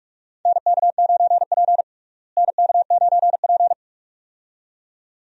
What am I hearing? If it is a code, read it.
Morse code, heard as a keyed tone, about 42 wpm, 700 Hz. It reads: NK9P NK9P
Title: USA Call Signs - 4 Characters - Repeated 2x - 45wpm